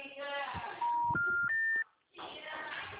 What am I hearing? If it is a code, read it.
Special information tone, with speech.